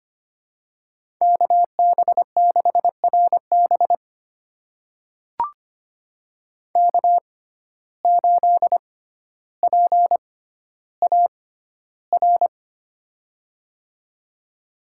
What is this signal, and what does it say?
Morse code, heard as a keyed tone, about 25 wpm, spaced as for 13 wpm, 700 Hz. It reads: KB6RB K8PAR
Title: USA Call Signs - 5 Characters - 6x Character Spacing - 25wpm